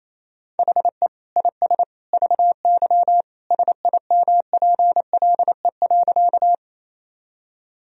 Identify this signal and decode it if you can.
Morse code, heard as a keyed tone, about 28 wpm, 700 Hz. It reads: HE IS VY SIMPLE.
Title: Sentences from Top 400 Words and Abbreviations 28wpm